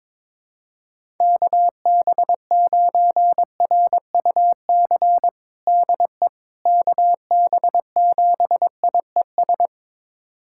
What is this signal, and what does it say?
Morse code, heard as a keyed tone, about 22 wpm, 700 Hz. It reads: KB9RUC DE KB7IES
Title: USA Callsign DE Callsign - 6 Characters - 22wpm